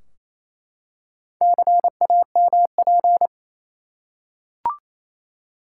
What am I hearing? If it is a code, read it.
Morse code, heard as a keyed tone, about 28 wpm, 700 Hz. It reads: CAMP
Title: Four Letter Words - 28wpm